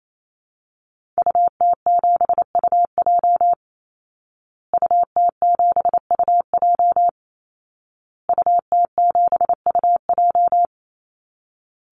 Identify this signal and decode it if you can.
Morse code, heard as a keyed tone, about 28 wpm, 700 Hz. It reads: UT7UJ UT7UJ UT7UJ
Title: Call Signs Repeated 2x - 28wpm